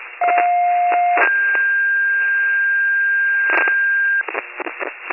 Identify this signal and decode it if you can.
Two-tone page, heard as a one-second tone, then a three-second tone, about 710 then 1600 Hz, with noise. It is heard from 0.2 to 4.2 s.